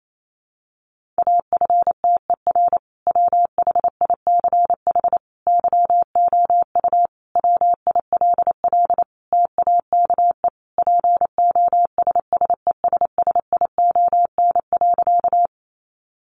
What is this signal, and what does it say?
Morse code, heard as a keyed tone, about 28 wpm, 700 Hz. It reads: AFTER WHICH YOU WILL TAKE POSSESSION.
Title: Sentences from Top 1400 Words 28wpm